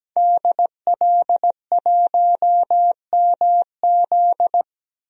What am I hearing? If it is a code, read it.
Morse code, heard as a keyed tone, about 17 wpm, 700 Hz. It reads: DL1MZ
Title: DXCC-100 Call Signs - 5 Characters - 17wpm